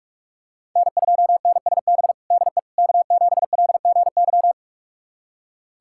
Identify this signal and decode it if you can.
Morse code, heard as a keyed tone, about 43 wpm, 700 Hz. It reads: N2NSB DE K7LGY
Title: USA Callsign DE Callsign - 5 Characters - 45wpm